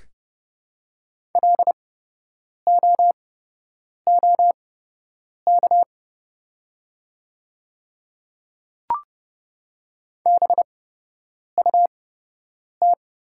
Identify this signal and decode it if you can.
Morse code, heard as a keyed tone, about 30 wpm, spaced as for 8 wpm, 700 Hz. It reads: LOOK BUT
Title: 8x Character Spacing - Top 100 Words - 30wpm